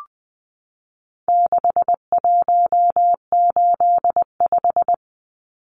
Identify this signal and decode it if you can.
Morse code, heard as a keyed tone, about 20 wpm, 700 Hz. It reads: 6185
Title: Numbers - 4-Digit 20wpm